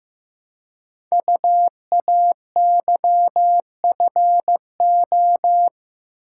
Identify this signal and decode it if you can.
Morse code, heard as a keyed tone, about 15 wpm, 700 Hz. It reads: UAYFO
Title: Code Groups - 5 Letters - 15wpm